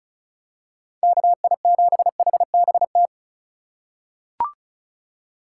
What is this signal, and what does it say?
Morse code, heard as a keyed tone, about 36 wpm, 700 Hz. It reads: KI7HBT
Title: USA Call Signs - 6 Characters - 4x Character Spacing - 35wpm